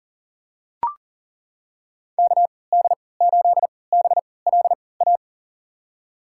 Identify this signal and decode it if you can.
Morse code, heard as a keyed tone, about 40 wpm, spaced as for 23 wpm, 700 Hz. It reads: KD8BLA
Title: USA Call Signs - 3x Character Spacing - 40wpm